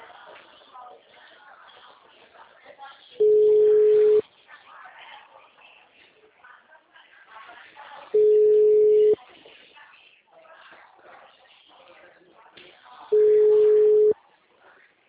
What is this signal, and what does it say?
Ringback tone (Europe, 425 Hz), with speech.